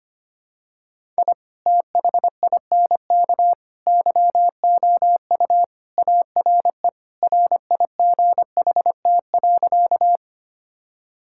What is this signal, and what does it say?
Morse code, heard as a keyed tone, about 25 wpm, 700 Hz. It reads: I THINK YOU ARE RIGHT.
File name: practice / 25wpm / Sentences with 3-Word Phrases - 25wpm